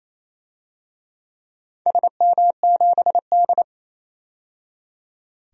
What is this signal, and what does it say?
Morse code, heard as a keyed tone, about 29 wpm, 690 Hz. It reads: SM7D
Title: DXCC-100 Call Signs - 4 Characters - 28wpm